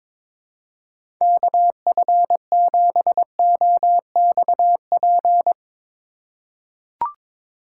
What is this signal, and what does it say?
Morse code, heard as a keyed tone, about 22 wpm, 700 Hz. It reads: KF7OXP
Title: USA Call Signs - 6 Characters - 4x Character Spacing - 22wpm